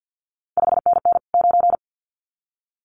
Morse code, heard as a keyed tone, about 50 wpm, 700 Hz. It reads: 5NN 9
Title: CQ WW DX Contest Exchange - 50wpm